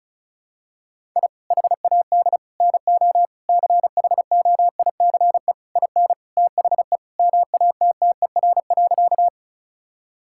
Morse code, heard as a keyed tone, about 36 wpm, 700 Hz. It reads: I HAD NO CHOICE IN THE MATTER.
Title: Sentences from Top 1000 Words - Review Part 6 - 35wpm